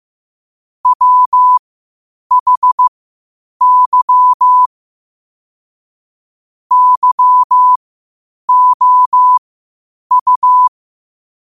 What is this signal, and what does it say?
Morse code, heard as a keyed tone, about 15 wpm, spaced as for 8 wpm, 1000 Hz. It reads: WHY YOU
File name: qotdfiles / 8wpm